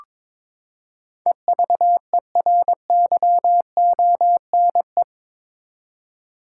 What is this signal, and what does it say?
Morse code, heard as a keyed tone, about 21 wpm, 700 Hz. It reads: EVERYONE
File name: practice / 22wpm / Eight Letter Words - 22wpm